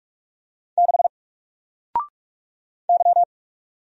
Morse code, heard as a keyed tone, about 44 wpm, 700 Hz. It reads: 6 Y
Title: Single Character (Letter:Number:Punctuation:Prosign) 45wpm